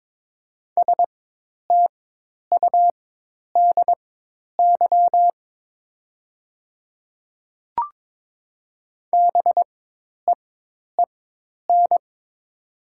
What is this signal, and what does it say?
Morse code, heard as a keyed tone, about 22 wpm, spaced as for 10 wpm, 700 Hz. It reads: STUDY BEEN
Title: 4x Character Spacing - Top 300 Words - 22wpm